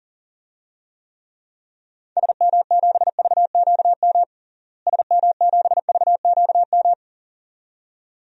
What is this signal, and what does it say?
Morse code, heard as a keyed tone, about 40 wpm, 700 Hz. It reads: SM7VQM SM7VQM
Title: DXCC-100 Call Signs - 6 Characters - Repeated 2x - 40wpm